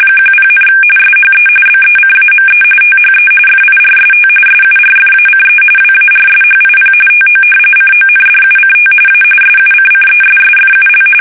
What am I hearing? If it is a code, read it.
RTTY, about 45.45 baud. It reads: 10, SHE BECAME THE FIRST SURFACE WARSHIP TO CONDUCT A SUCCESSFUL
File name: fsk850 / battleshipnj